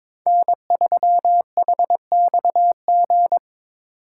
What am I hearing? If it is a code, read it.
Morse code, heard as a keyed tone, about 21 wpm, 700 Hz. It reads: N3HXG
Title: USA Call Signs - 5 Characters - 4x Character Spacing - 22wpm